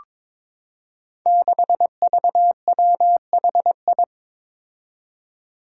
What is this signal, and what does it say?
Morse code, heard as a keyed tone, about 23 wpm, 700 Hz. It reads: 6VWHI
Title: Five Challenging Letter-Number Combinations 22wpm